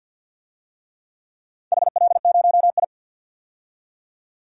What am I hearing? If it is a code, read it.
Morse code, heard as a keyed tone, about 51 wpm, 700 Hz. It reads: HL0I
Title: DXCC-100 Call Signs - 4 Characters - 50wpm